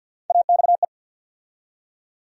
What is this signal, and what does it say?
Morse code, heard as a keyed tone, about 52 wpm, 700 Hz. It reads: AXE
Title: USA Call Sign Postfixes - 3 Letters 50wpm